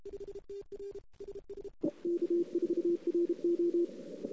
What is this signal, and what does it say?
Morse code, heard as a keyed tone, about 31 wpm, 340 Hz. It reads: K4RO
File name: CW / 2008-ss-cw-k5tr-op-n5rz -03:00